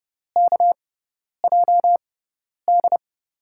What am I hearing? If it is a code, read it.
Morse code, heard as a keyed tone, about 30 wpm, spaced as for 10 wpm, 700 Hz. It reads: KJD
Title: USA Call Signs - 6x Character Spacing - 30wpm